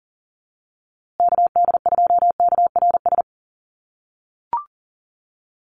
Morse code, heard as a keyed tone, about 40 wpm, 700 Hz. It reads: KD2KRS
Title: USA Call Signs Repeated 2x 40wpm